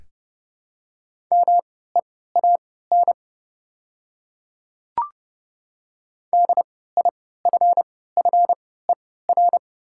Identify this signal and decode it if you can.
Morse code, heard as a keyed tone, about 29 wpm, spaced as for 17 wpm, 700 Hz. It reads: MEAN DIFFER
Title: 3x Character Spacing - Top 200 Words - 30wpm